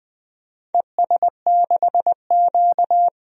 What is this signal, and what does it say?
Morse code, heard as a keyed tone, about 20 wpm, 700 Hz. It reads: ES6Q
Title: DXCC-100 Call Signs - 4 Characters - 20wpm